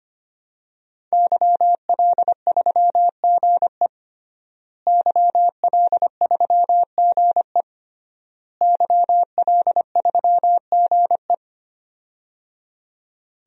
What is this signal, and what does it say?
Morse code, heard as a keyed tone, about 25 wpm, 700 Hz. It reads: YL3GE YL3GE YL3GE
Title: DXCC-100 Call Signs - 5 Characters - Repeated 3x - 25wpm